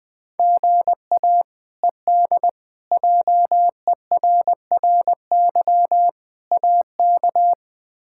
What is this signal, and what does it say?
Morse code, heard as a keyed tone, about 21 wpm, 700 Hz. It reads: GA ED JERRY AK
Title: K1USN SST QSO Practice - 20wpm